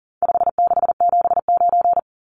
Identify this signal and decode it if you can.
Morse code, heard as a keyed tone, about 40 wpm, 700 Hz. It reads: 5679
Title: Numbers - 4-Digit 40wpm